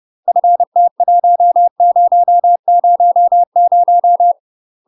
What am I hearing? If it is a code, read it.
Morse code, heard as a keyed tone, about 30 wpm, 700 Hz. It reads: FT1000
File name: UK MASTER QSOs TRAINING / 30WPM / UK MASTER QSOs 0003